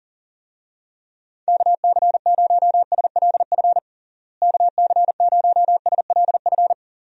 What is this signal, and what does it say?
Morse code, heard as a keyed tone, about 40 wpm, 700 Hz. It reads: KC0SLF KC0SLF
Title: USA Call Signs - 6 Characters - Repeated 3x - 40wpm